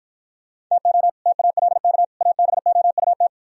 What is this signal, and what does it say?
Morse code, heard as a keyed tone, about 56 wpm, 700 Hz. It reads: TO TALK ABOUT